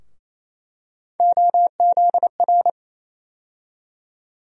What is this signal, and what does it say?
Morse code, heard as a keyed tone, about 28 wpm, 700 Hz. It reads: OZR
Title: USA Call Sign Postfixes - 3 Letters 28wpm